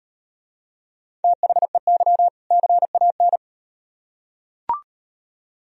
Morse code, heard as a keyed tone, about 38 wpm, 700 Hz. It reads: THEY CAN